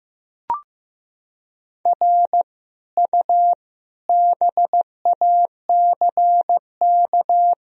Morse code, heard as a keyed tone, about 15 wpm, 700 Hz. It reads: R U BACK
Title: Sentences from Top 200 Words and Abbreviations 15wpm